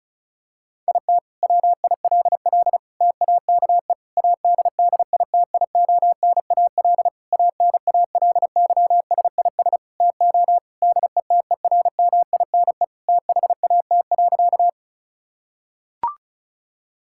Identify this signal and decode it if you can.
Morse code, heard as a keyed tone, about 36 wpm, 700 Hz. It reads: IT WILL TAKE ADDITIONAL ANALYSIS TO DETERMINE THAT.
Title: Sentences from Top 1000 Words - Review Part 5 - 35wpm